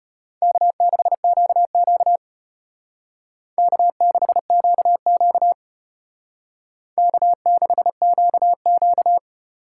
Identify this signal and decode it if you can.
Morse code, heard as a keyed tone, about 31 wpm, 700 Hz. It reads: K6QQ K6QQ K6QQ
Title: Speed Racing - USA 1x2 Call Signs 25wpm